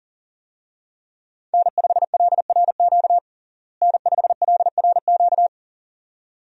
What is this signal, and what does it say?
Morse code, heard as a keyed tone, about 37 wpm, 700 Hz. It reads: N5LRQ N5LRQ
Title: USA Call Signs - 5 Characters - Repeated 2x - 40wpm